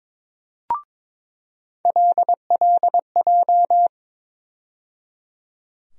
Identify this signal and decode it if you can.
Morse code, heard as a keyed tone, about 22 wpm, 700 Hz. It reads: LLJ
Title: USA Call Sign Postfixes - 3 Letters 22wpm